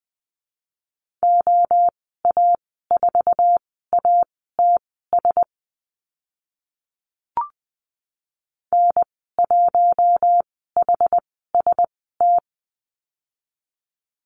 Morse code, heard as a keyed tone, about 20 wpm, spaced as for 15 wpm, 700 Hz. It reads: OA4ATS N1HST
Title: 076 Lesson 13g - Letter L - Call Signs 20wpm